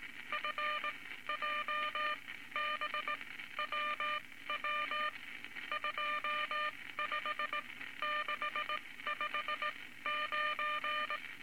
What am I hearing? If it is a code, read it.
Morse code, heard as a keyed tone, about 19 wpm, spaced as for 14 wpm, 1200 Hz, with noise. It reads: FJBWW 25659
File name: Morse Code Records / 19 A LCM 166 Ten Words Per Minute